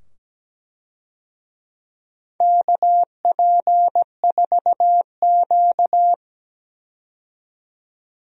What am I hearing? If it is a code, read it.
Morse code, heard as a keyed tone, about 17 wpm, 700 Hz. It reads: KP4Q
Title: DXCC-100 Call Signs - 4 Characters - 17wpm